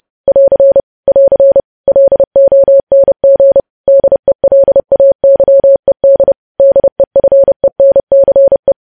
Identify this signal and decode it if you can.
Morse code, heard as a keyed tone, about 30 wpm, 550 Hz. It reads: <AR> <AR> LONG DELAYED DEFENCE
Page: news-headlines-in-morse-code-at-30-wpm